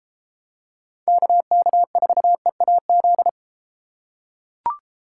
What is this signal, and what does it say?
Morse code, heard as a keyed tone, about 32 wpm, 700 Hz. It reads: KK4EAZ